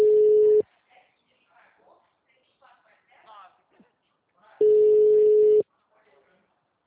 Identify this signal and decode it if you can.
Ringback tone (Europe, 425 Hz), with speech.